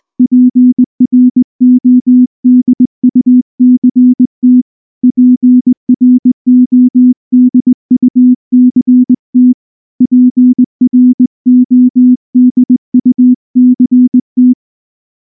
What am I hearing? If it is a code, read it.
Morse code, heard as a keyed tone, about 19 wpm, 250 Hz. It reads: PRODUCT PRODUCT PRODUCT
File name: morse / morse-training-20WPM-3